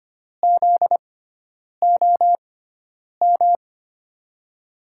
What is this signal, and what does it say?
Morse code, heard as a keyed tone, about 25 wpm, spaced as for 9 wpm, 700 Hz. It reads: ZOM
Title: USA Call Signs - 6 Characters - 6x Character Spacing - 25wpm